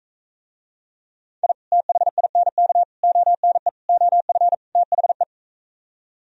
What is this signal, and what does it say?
Morse code, heard as a keyed tone, about 45 wpm, 700 Hz. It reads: I THINK ONE OF THE